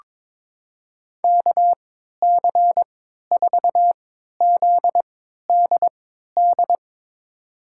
Morse code, heard as a keyed tone, about 22 wpm, spaced as for 13 wpm, 700 Hz. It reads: KC4ZDD
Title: USA Call Signs - 6 Characters - 3x Character Spacing - 22wpm